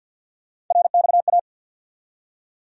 Morse code, heard as a keyed tone, about 52 wpm, 700 Hz. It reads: RXA
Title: USA Call Sign Postfixes - 3 Letters 50wpm